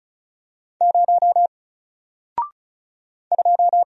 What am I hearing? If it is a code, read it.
Morse code, heard as a keyed tone, about 36 wpm, 700 Hz. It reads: 0 2
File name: Single-Letter-Number-35wpm